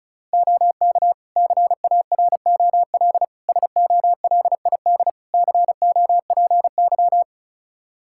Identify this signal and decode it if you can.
Morse code, heard as a keyed tone, about 34 wpm, 700 Hz. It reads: OK CAROL SOLID COPY
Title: K1USN SST QSO Practice 35wpm